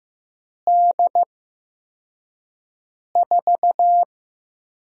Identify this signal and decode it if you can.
Morse code, heard as a keyed tone, about 15 wpm, spaced as for 4 wpm, 700 Hz. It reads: D4
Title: USA Call Signs - 8x Character Spacing - 15wpm